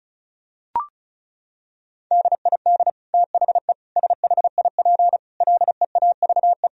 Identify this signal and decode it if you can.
Morse code, heard as a keyed tone, about 34 wpm, 700 Hz. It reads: DID THE SHIP LEAVE
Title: Sentences from Top 1000 Words - Review Part 2 - 35wpm